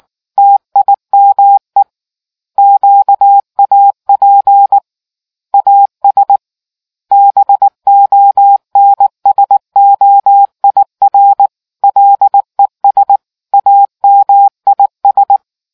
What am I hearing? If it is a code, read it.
Morse code, heard as a keyed tone, about 19 wpm, 800 Hz. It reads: TIME QAP AS BONSOIR LES AMIS